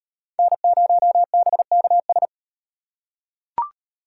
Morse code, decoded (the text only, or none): N0BKS